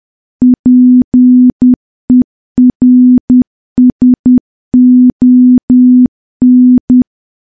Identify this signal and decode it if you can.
Morse code, heard as a keyed tone, about 10 wpm, 250 Hz. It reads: PERSON
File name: morse / morse-training-10WPM